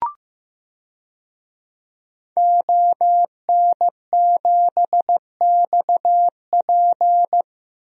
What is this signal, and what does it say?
Morse code, heard as a keyed tone, about 15 wpm, 700 Hz. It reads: ON7XP
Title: DXCC-100 Call Signs - 5 Characters - Repeated 2x - 15wpm